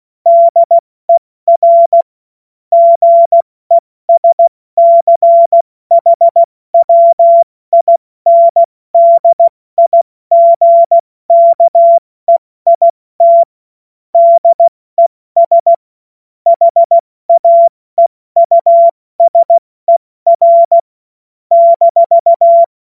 Morse code, heard as a keyed tone, about 16 wpm, 680 Hz. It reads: DER GESCHWINDIGKEIT DES HAEUSER -
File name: Morsekurs / Witze / Witz 30. Dezember 2025 70 F80